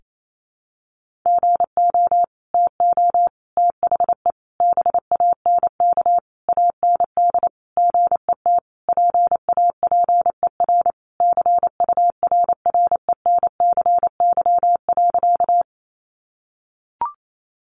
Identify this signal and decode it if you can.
Morse code, heard as a keyed tone, about 28 wpm, 700 Hz. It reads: GO TO THE BANK AND GET PAPER CURRENCY.
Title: Sentences from Top 1500 Words 28wpm